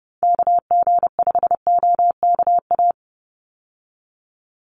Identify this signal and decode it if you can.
Morse code, heard as a keyed tone, about 30 wpm, 700 Hz. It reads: KG5OKA
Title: USA Call Signs 30wpm